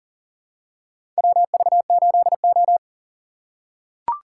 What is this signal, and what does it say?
Morse code, heard as a keyed tone, about 40 wpm, 700 Hz. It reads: WV8O
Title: USA Call Signs - 4 Characters - 3x Character Spacing - 40wpm